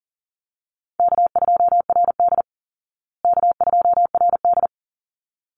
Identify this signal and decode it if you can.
Morse code, heard as a keyed tone, about 40 wpm, 700 Hz. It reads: K2RD K2RD
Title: Easy Call Signs Repeated 2x - 40wpm